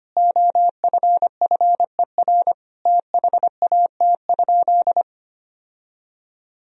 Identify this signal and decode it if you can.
Morse code, heard as a keyed tone, about 25 wpm, 700 Hz. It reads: OFFER THAT?
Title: Sentences from Top 1000 Words - Review Part 4 - 25wpm